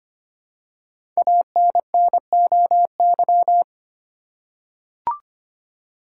Morse code, heard as a keyed tone, about 25 wpm, 700 Hz. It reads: ANNOY
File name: 5 Letter Words - First 3 Letters Given - 25wpm